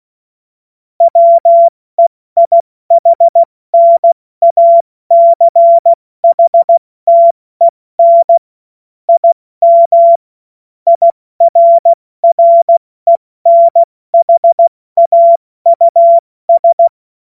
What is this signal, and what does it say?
Morse code, heard as a keyed tone, about 16 wpm, 680 Hz. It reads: WEIHNACHTEN IM IRRENHAUS
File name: Morsekurs / Witze / Witz 09. November 2021, 70 F80